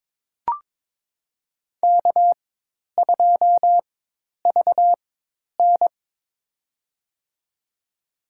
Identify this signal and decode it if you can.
Morse code, heard as a keyed tone, about 22 wpm, spaced as for 10 wpm, 700 Hz. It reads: K2VN
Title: USA Call Signs - 4 Characters - 4x Character Spacing - 22wpm